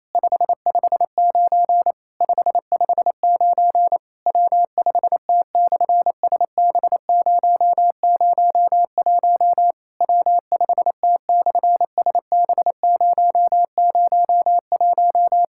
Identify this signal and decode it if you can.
Morse code, heard as a keyed tone, about 28 wpm, 700 Hz. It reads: 559 559 W5T/SB001 W5T/SB001
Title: SOTA QSO Practice 28wpm